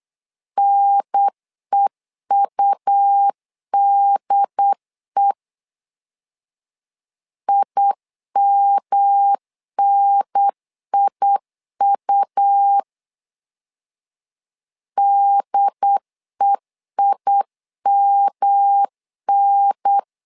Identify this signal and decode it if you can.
Morse code, heard as a keyed tone, about 8 wpm, 800 Hz. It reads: NEUDE IMNIU DEIMN
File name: LEZIONI / 3 MWRO / 3 M 8wpm 2300